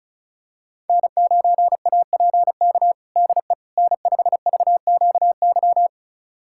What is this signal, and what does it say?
Morse code, heard as a keyed tone, about 36 wpm, 700 Hz. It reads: N9APK DE N5VQY
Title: USA Callsign DE Callsign - 5 Characters - 35wpm